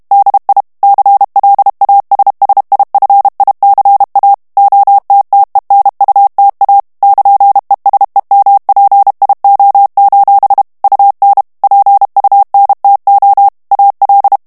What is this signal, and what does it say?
Morse code, heard as a keyed tone, about 32 wpm, 800 Hz. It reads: DI CLASSIFICA OTTENUTA <KN>ESEMPIO: UN PUNTO AL
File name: ITALIEN texte clair 2 vitesse 30 mots minute